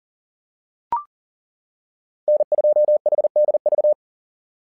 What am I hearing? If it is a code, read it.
Morse code, heard as a keyed tone, about 40 wpm, 600 Hz. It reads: N2HDV
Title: SOTA QSO Practice 40wpm